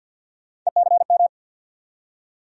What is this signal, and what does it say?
Morse code, heard as a keyed tone, about 47 wpm, 700 Hz. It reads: ECM
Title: USA Call Sign Postfixes - 3 Letters 50wpm